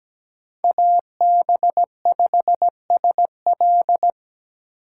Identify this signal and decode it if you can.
Morse code, heard as a keyed tone, about 17 wpm, 700 Hz. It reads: AB5SL
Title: USA Call Signs - 5 Characters - 2x Character Spacing - 17wpm